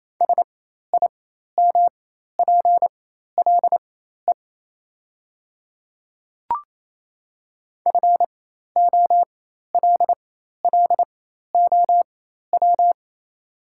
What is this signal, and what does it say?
Morse code, heard as a keyed tone, about 28 wpm, spaced as for 13 wpm, 700 Hz. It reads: SIMPLE FOLLOW
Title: 4x Character Spacing - Top 500 Words - 28wpm